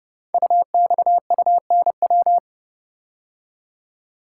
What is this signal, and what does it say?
Morse code, heard as a keyed tone, about 28 wpm, 700 Hz. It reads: UXUNW
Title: Code Groups - 5 Letters - 30wpm